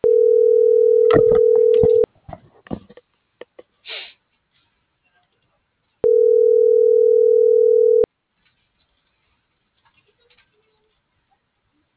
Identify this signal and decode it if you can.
Ringback tone (North America), 440 + 480 Hz.